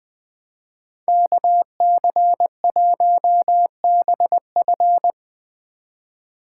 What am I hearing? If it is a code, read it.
Morse code, heard as a keyed tone, about 20 wpm, 700 Hz. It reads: KC1BF
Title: Speed Racing - USA 2x2 Call Signs 17wpm